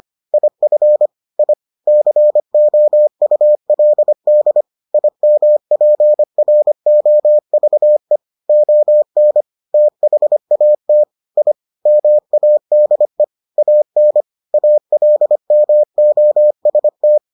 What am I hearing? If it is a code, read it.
Morse code, heard as a keyed tone, about 25 wpm, 600 Hz. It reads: IF I COULD IMPROVE ON THAT I MADE AN ALMOST